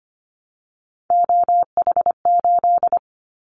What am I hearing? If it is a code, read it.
Morse code, heard as a keyed tone, about 25 wpm, 700 Hz. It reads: OH8
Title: Callsign Prefixes for Top 25 DXCC - 25wpm